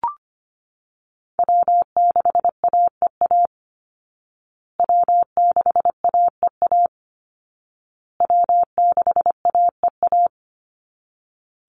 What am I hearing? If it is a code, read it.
Morse code, heard as a keyed tone, about 25 wpm, 700 Hz. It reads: W6AEA W6AEA W6AEA
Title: Call Signs Repeated 2x - 25wpm